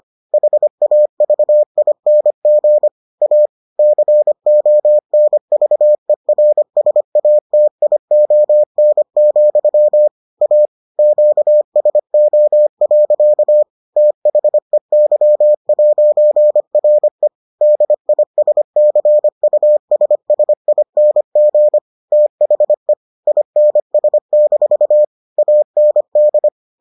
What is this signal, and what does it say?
Morse code, heard as a keyed tone, about 25 wpm, 600 Hz. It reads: HAVING A CONVERSATION, A QSO. THEY'RE DISCUSSING THE INS- AND